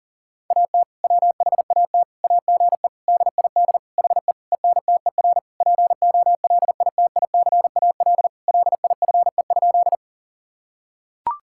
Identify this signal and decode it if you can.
Morse code, heard as a keyed tone, about 40 wpm, 700 Hz. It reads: AT WHAT AGE DID HE ENTER POLITICAL LIFE?
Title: Sentences from Top 1000 Words - Review Part 5 - 40wpm